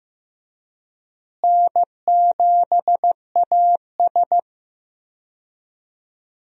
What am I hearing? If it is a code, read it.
Morse code, heard as a keyed tone, about 16 wpm, 700 Hz. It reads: N7AS
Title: USA Call Signs - 4 Characters - Repeated 3x - 15wpm